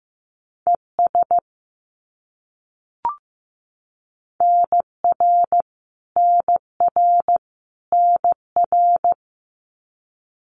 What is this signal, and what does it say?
Morse code, heard as a keyed tone, about 15 wpm, 700 Hz. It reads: ES NR NR NR
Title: Abbreviations Repeated 3x - 15wpm